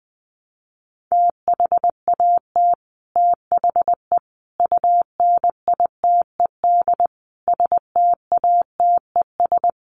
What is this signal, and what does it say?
Morse code, heard as a keyed tone, about 20 wpm, 700 Hz. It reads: THAT THE UNITED STATES
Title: Sets of 5 Words Encore 20wpm